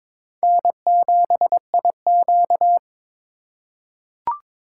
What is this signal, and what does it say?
Morse code, heard as a keyed tone, about 21 wpm, 700 Hz. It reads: N7IQ
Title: USA Call Signs - 4 Characters - 4x Character Spacing - 22wpm